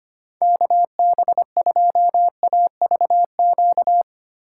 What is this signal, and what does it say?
Morse code, heard as a keyed tone, about 25 wpm, 700 Hz. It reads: KB2AVQ
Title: USA Call Signs - 6 Characters - 6x Character Spacing - 25wpm